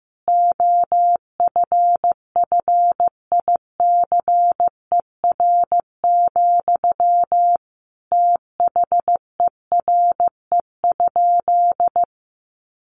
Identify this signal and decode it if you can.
Morse code, heard as a keyed tone, about 15 wpm, 700 Hz. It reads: OFFICER, THERE?
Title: Sentences from Top 1200 Words 15wpm